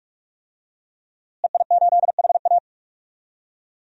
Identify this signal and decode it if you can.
Morse code, heard as a keyed tone, about 46 wpm, 700 Hz. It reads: EI8HA
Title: DXCC-100 Call Signs - 5 Characters - Repeated 3x - 45wpm